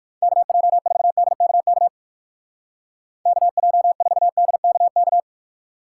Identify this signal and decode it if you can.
Morse code, heard as a keyed tone, about 55 wpm, 700 Hz. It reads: KJ4DKK KJ4DKK
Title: Speed Racing - USA 2x3 Call Signs 45wpm